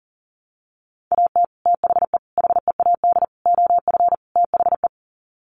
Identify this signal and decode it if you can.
Morse code, heard as a keyed tone, about 40 wpm, 700 Hz. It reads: AT THE HEAD OF THE
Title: Sets of 5 Words 40wpm